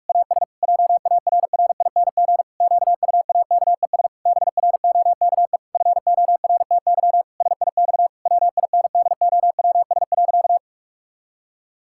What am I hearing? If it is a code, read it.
Morse code, heard as a keyed tone, about 46 wpm, 700 Hz. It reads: AS JARRING QUAKES BROKE FORTY SIX WINDOWS.
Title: Pangram Challenge v2 - 45wpm